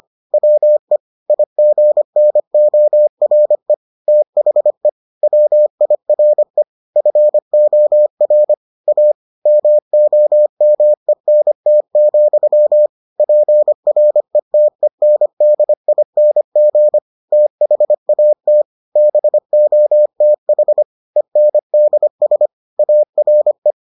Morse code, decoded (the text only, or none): WE IGNORE THE WIRE FOR A MOMENT, PRETENDING THAT BOTH ENDS ARE